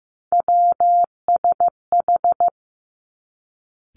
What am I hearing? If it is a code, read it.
Morse code, heard as a keyed tone, about 15 wpm, 700 Hz. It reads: WSH
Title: Three Challenging Letter-Number Combinations 15wpm